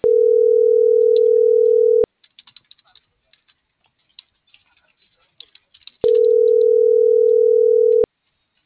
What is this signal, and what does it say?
Ringback tone (North America), 440 + 480 Hz.